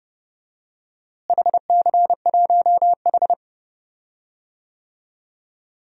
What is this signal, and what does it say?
Morse code, heard as a keyed tone, about 30 wpm, 700 Hz. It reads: HC1H
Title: DXCC-100 Call Signs - 4 Characters - Repeated 2x - 30wpm